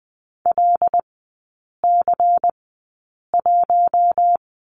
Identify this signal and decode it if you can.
Morse code, heard as a keyed tone, about 20 wpm, spaced as for 8 wpm, 700 Hz. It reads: LC1
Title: 198 Lesson 32e - Letter k - Three Characters 20wpm